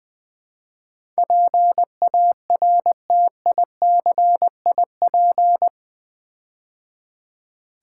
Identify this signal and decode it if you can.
Morse code, heard as a keyed tone, about 20 wpm, 700 Hz. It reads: PARTICIP
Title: Build A Word - 4 Letters At A Time - 11-Letter Words - 20wpm